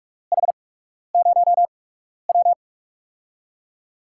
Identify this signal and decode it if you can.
Morse code, heard as a keyed tone, about 44 wpm, spaced as for 12 wpm, 700 Hz. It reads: H0W
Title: USA Call Signs - 8x Character Spacing - 45wpm